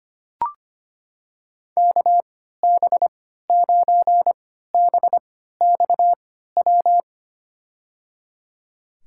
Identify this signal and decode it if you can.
Morse code, heard as a keyed tone, about 25 wpm, spaced as for 14 wpm, 700 Hz. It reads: KB9BXW
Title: USA Call Signs - 6 Characters - 3x Character Spacing - 25wpm